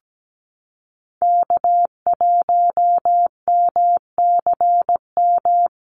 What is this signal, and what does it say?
Morse code, heard as a keyed tone, about 17 wpm, 700 Hz. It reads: K1MCM
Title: USA Call Signs Repeated 3x 17wpm